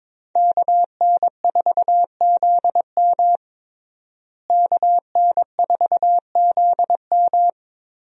Morse code, heard as a keyed tone, about 22 wpm, 700 Hz. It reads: KN4ZM KN4ZM
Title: USA Call Signs - 5 Characters - Repeated 2x - 22wpm